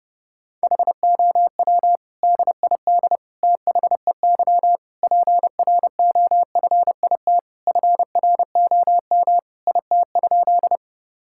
Morse code, heard as a keyed tone, about 30 wpm, 700 Hz. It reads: HOW DID THEY PROFIT FROM IT?
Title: Sentences from Top 1000 Words - Review Part 5 - 30wpm